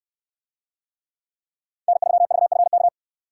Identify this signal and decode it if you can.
Morse code, heard as a keyed tone, about 67 wpm, 700 Hz. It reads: N3VFK